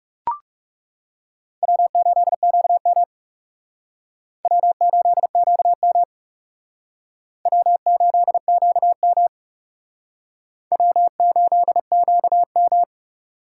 Morse code, heard as a keyed tone, about 39 wpm, 700 Hz. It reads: W8QM W8QM W8QM W8QM